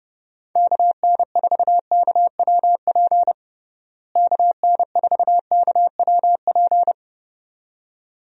Morse code, heard as a keyed tone, about 32 wpm, 700 Hz. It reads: KN4KWP KN4KWP
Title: USA Call Signs - 6 Characters - Repeated 2x - 30wpm